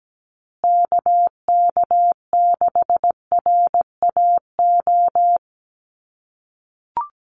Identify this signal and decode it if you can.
Morse code, heard as a keyed tone, about 17 wpm, 700 Hz. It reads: KK6RAO
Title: USA Call Signs Repeated 3x 17wpm